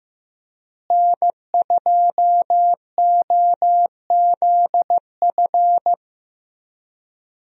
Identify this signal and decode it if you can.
Morse code, heard as a keyed tone, about 16 wpm, 700 Hz. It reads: N2OZF
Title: USA Call Signs - 5 Characters - Repeated 2x - 15wpm